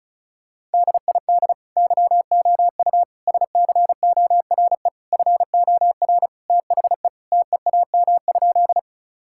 Morse code, heard as a keyed tone, about 34 wpm, 700 Hz. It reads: DID YOU SCORE FOR THE TEAM?
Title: Sentences from Top 1600 Words 35wpm